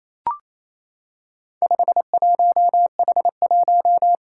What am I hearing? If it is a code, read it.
Morse code, heard as a keyed tone, about 27 wpm, 700 Hz. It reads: 51H1
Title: Four Challenging Letter-Number Combinations 28wpm